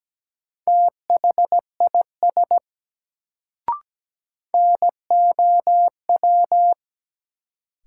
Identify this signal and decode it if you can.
Morse code, heard as a keyed tone, about 17 wpm, 700 Hz. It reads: THIS NOW
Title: Sets of 2 Words - New Word Drill - 17wpm